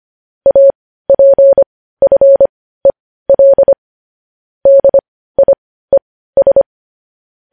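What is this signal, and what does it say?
Morse code, heard as a keyed tone, about 25 wpm, spaced as for 15 wpm, 550 Hz. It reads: APFEL DIES
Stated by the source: News headlines sent in Morse code at 15 words per minute, preceded by vvv.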